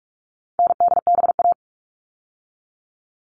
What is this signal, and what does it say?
Morse code, heard as a keyed tone, about 44 wpm, 700 Hz. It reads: NDBA
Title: All Cut Numbers - 4-Digit 45wpm